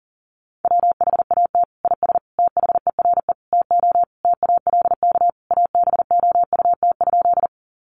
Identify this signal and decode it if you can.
Morse code, heard as a keyed tone, about 37 wpm, 700 Hz. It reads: WHAT IS THERE TO TALK ABOUT?
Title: Sentences from Top 400 Words 40wpm